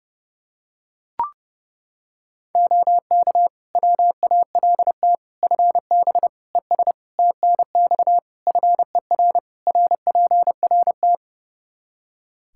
Morse code, heard as a keyed tone, about 32 wpm, 700 Hz. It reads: OK WALT FB ES TNX FER RPRT
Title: Practice QSO - Exchange 2 - Info - 30wpm